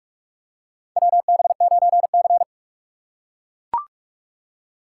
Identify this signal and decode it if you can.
Morse code, heard as a keyed tone, about 42 wpm, 700 Hz. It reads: WB9C
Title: USA Call Signs - 6x Character Spacing - 45wpm